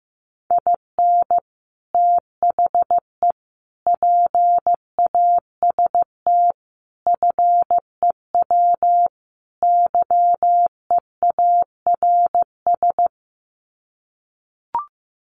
Morse code, heard as a keyed tone, about 15 wpm, 700 Hz. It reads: IN THE PAST FEW YEARS
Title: Sets of 5 Words 15wpm